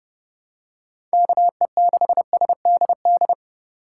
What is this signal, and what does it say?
Morse code, heard as a keyed tone, about 30 wpm, 700 Hz. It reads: KE6SDD
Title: USA Call Signs - 3x Character Spacing - 30wpm